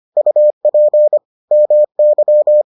Morse code, heard as a keyed tone, about 25 wpm, 600 Hz. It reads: UP MY